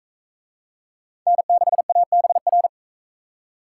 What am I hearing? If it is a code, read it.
Morse code, heard as a keyed tone, about 44 wpm, 700 Hz. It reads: N6ABR